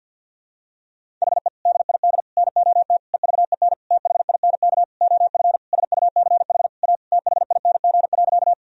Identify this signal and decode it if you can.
Morse code, heard as a keyed tone, about 51 wpm, 700 Hz. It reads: HE DID NOT EVEN THINK OF SUCH A THING.
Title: Sentences from Top 1000 Words - Review Part 1 - 50wpm